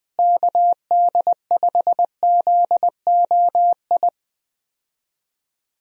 Morse code, heard as a keyed tone, about 20 wpm, 700 Hz. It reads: KD5ZOI
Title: USA Call Signs - 6 Characters - Repeated 3x - 20wpm